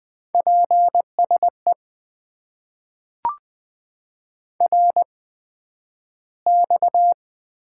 Morse code, heard as a keyed tone, about 20 wpm, spaced as for 11 wpm, 700 Hz. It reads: PSE RX
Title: Top 90 QSO Elements - 8x Character Spacing - 20wpm